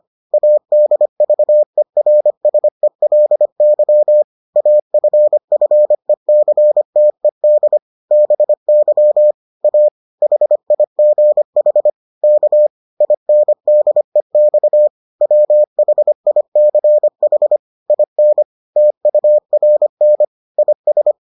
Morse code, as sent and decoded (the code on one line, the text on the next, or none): .- -.. ...- . .-. ... .-.. -.-- / .- ..-. ..-. -.-. - . -.. / -... -.-- / .- / .... .. --. .... / -.- / .. -. -.. -..- / .-- .... .. -.-. .... / .. -. / - ..- .-. -. / .. ...
ADVERSELY AFFECTED BY A HIGH K INDEX WHICH IN TURN IS